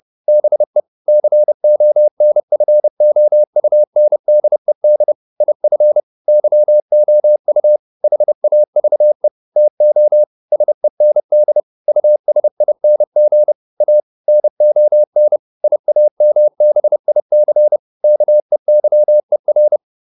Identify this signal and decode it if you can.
Morse code, heard as a keyed tone, about 30 wpm, 600 Hz. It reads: BE CONFOUNDED IF YOU HAVE TO SEND USING A NON IAMBIC KEYER